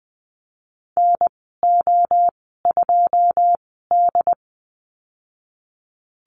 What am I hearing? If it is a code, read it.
Morse code, heard as a keyed tone, about 20 wpm, spaced as for 15 wpm, 700 Hz. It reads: NO2D
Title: 167 Lesson 27g - Number 7 - Call Signs 20wpm